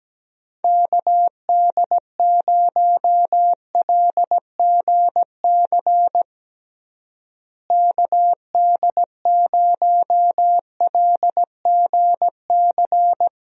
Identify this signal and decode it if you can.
Morse code, heard as a keyed tone, about 17 wpm, 700 Hz. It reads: KD0LGC KD0LGC
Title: USA Call Signs - 6 Characters - Repeated 2x - 17wpm